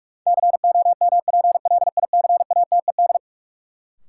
Morse code, heard as a keyed tone, about 42 wpm, 700 Hz. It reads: COMPLICATED
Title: Sentences from Top 1200 Words - New Word Drill - 45wpm